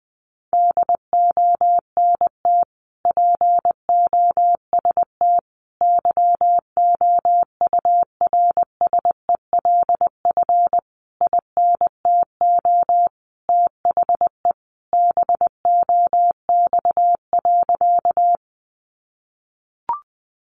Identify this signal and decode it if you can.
Morse code, heard as a keyed tone, about 20 wpm, 700 Hz. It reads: DONT POST YOURSELF INTO THE BOX.
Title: Sentences from Top 1000 Words 20wpm